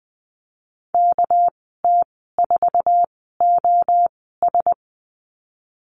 Morse code, heard as a keyed tone, about 20 wpm, spaced as for 14 wpm, 700 Hz. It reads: KT4OS
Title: 231 Lesson 37g - Letter x - Call Signs 20wpm